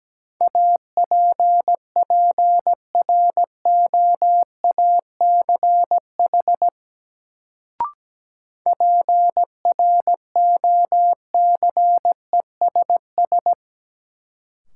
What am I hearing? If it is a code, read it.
Morse code, heard as a keyed tone, about 17 wpm, 700 Hz. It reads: APPROACH PROCESS
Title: Sentences from Top 1400 Words - New Word Drill Alt - 17wpm